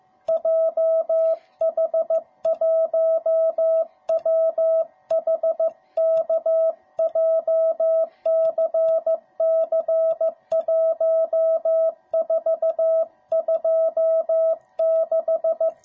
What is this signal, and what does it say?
Morse code, heard as a keyed tone, about 15 wpm, 660 Hz. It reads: JH1WHK JCC1426